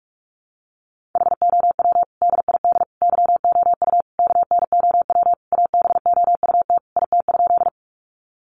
Morse code, heard as a keyed tone, about 44 wpm, 700 Hz. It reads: HOW DID YOU KNOW ABOUT IT?
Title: Sentences from Top 100 Words 45wpm